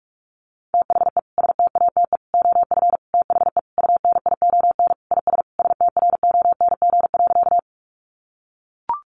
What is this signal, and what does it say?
Morse code, heard as a keyed tone, about 44 wpm, 700 Hz. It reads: THE STATE OF THE UNION IS STRONG.
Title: Sentences from Top 1400 Words 45wpm